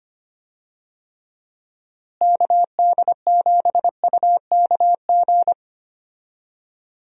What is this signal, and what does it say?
Morse code, heard as a keyed tone, about 25 wpm, 700 Hz. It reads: KD7UKG
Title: USA Call Signs - 6 Characters - Repeated 2x - 25wpm